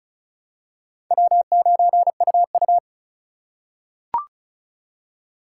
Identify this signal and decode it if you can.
Morse code, heard as a keyed tone, about 34 wpm, 700 Hz. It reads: W9UU
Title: USA Call Signs - 4 Characters - 2x Character Spacing - 35wpm